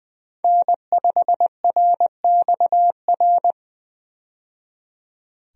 Morse code, heard as a keyed tone, about 20 wpm, 700 Hz. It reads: N5RXR
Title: USA Call Signs - 5 Characters - 20wpm